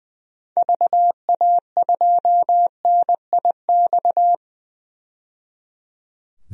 Morse code, heard as a keyed tone, about 20 wpm, 700 Hz. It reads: VA2NIX
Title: DXCC-100 Call Signs - 6 Characters - 20wpm